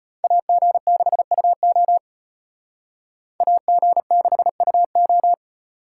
Morse code, heard as a keyed tone, about 38 wpm, 700 Hz. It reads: AG6UO AG6UO